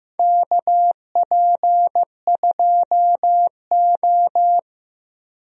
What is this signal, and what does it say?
Morse code, heard as a keyed tone, about 15 wpm, 700 Hz. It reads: KP2O
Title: DXCC-100 Call Signs - 4 Characters - Repeated 3x - 15wpm